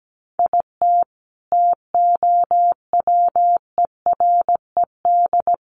Morse code, heard as a keyed tone, about 17 wpm, 700 Hz. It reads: IT TOWERED
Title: A Sound Of Thunder 17wpm